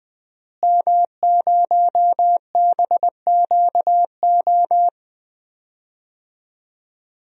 Morse code, heard as a keyed tone, about 20 wpm, 700 Hz. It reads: M0BQO
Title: DXCC-100 Call Signs - 5 Characters - Repeated 2x - 20wpm